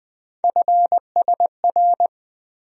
Morse code, heard as a keyed tone, about 20 wpm, 700 Hz. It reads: FSR
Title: USA Call Sign Postfixes - 3 Letters 20wpm